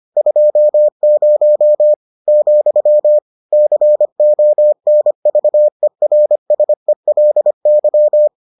Morse code, heard as a keyed tone, about 25 wpm, 600 Hz. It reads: 20, CONVERSELY